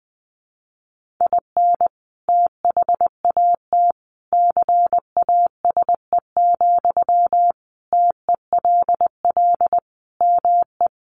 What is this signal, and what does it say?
Morse code, heard as a keyed tone, about 20 wpm, 700 Hz. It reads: IN THAT CASE, TELL ME
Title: Sentences from Top 1400 Words 20wpm